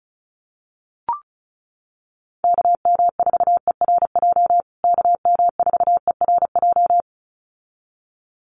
Morse code, heard as a keyed tone, about 34 wpm, 700 Hz. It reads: KM4ERJ KM4ERJ
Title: USA Call Signs Repeated 2x 35wpm